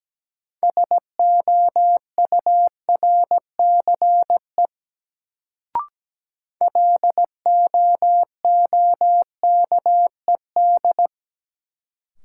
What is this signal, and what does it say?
Morse code, heard as a keyed tone, about 17 wpm, 700 Hz. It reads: SOURCE LOOKED
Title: Sentences from Top 800 Words - New Word Drill - 17wpm